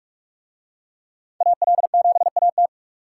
Morse code, heard as a keyed tone, about 40 wpm, 700 Hz. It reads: AL7AT